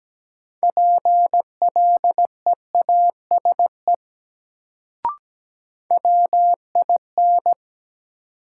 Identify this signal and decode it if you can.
Morse code, heard as a keyed tone, about 17 wpm, 700 Hz. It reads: PLEASE WIN
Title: Sentences from Top 1500 Words - New Word Drill Alt - 17wpm